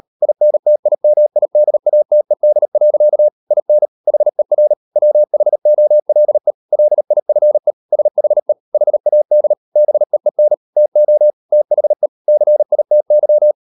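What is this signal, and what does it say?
Morse code, heard as a keyed tone, about 38 wpm, 600 Hz. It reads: INTIMIDATED. IN HER WHOLE LIFE SHE HAD BEEN TO THE CITY
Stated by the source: Morse code practice at 38 WPM, sending Ken Follett's novel Never in English, from CW med Gnister.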